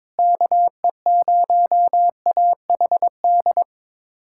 Morse code, heard as a keyed tone, about 22 wpm, 700 Hz. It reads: KE0AHD
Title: USA Callsign DE Callsign 22wpm